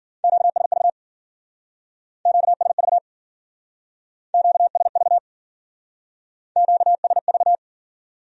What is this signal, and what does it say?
Morse code, heard as a keyed tone, about 49 wpm, 700 Hz. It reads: QSV QSV QSV QSV
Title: Speed-Racing Q-Codes - 40wpm